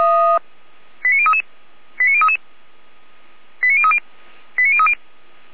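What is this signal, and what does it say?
Five-tone selective call (ZVEI), heard as quick runs of five tones, tones of about 70 ms. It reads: 89033 89033 89030 89030
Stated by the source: ZVEI five-tone call received by SDR, from the feuerwehr-orlamuende.org (fire brigade Orlamünde) server.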